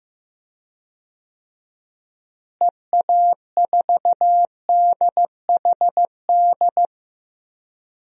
Morse code, heard as a keyed tone, about 15 wpm, 700 Hz. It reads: EA4DHD
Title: DXCC-100 Call Signs - 6 Characters - Repeated 2x - 15wpm